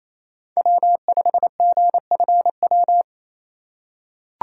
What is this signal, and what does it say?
Morse code, heard as a keyed tone, about 28 wpm, 700 Hz. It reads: W5GFW
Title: USA Call Signs - 5 Characters - 2x Character Spacing - 28wpm